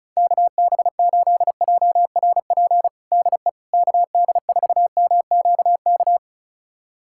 Morse code, heard as a keyed tone, about 36 wpm, 700 Hz. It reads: KB8JRP DE KD4MQK
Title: USA Callsign DE Callsign - 6 Characters - 35wpm